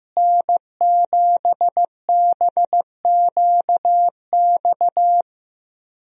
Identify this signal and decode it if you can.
Morse code, heard as a keyed tone, about 16 wpm, 700 Hz. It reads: N7BQX
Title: USA Call Signs - 5 Characters - Repeated 3x - 15wpm